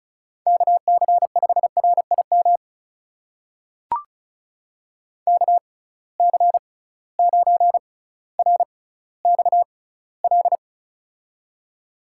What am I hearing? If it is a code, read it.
Morse code, heard as a keyed tone, about 36 wpm, spaced as for 18 wpm, 700 Hz. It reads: KC5RIM KC9RXL
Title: USA Call Signs - 6 Characters - 6x Character Spacing - 35wpm